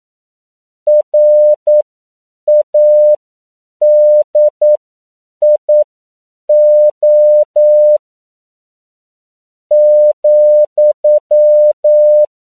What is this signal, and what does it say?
Morse code, heard as a keyed tone, about 9 wpm, spaced as for 7 wpm, 600 Hz. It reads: RADIO ,